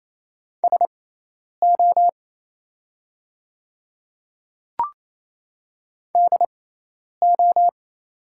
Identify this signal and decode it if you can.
Morse code, heard as a keyed tone, about 28 wpm, spaced as for 10 wpm, 700 Hz. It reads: SO DO
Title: 6x Character Spacing - Top 100 Words - 28wpm